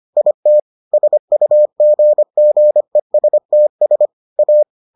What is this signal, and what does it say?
Morse code, heard as a keyed tone, about 25 wpm, 600 Hz. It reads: IT SUGGESTS A